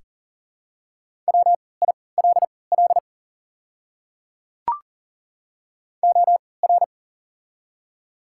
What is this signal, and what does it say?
Morse code, heard as a keyed tone, about 40 wpm, spaced as for 23 wpm, 700 Hz. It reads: WILL OR
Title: 3x Character Spacing - Top 100 Words - 40wpm